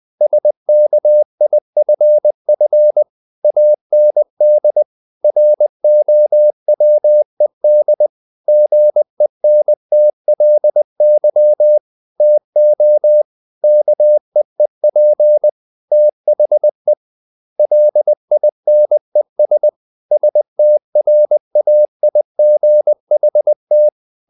Morse code, heard as a keyed tone, about 20 wpm, 600 Hz. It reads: SKIFF AND ROWED GENTLY TO KEEP THE LINES STRAIGHT
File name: the old man and the sea / the old man and the sea part 1 20wpm